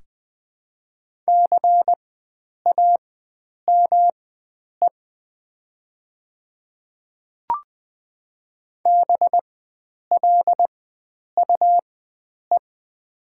Morse code, heard as a keyed tone, about 20 wpm, spaced as for 9 wpm, 700 Hz. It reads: CAME BLUE
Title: 4x Character Spacing - Top 500 Words - 20wpm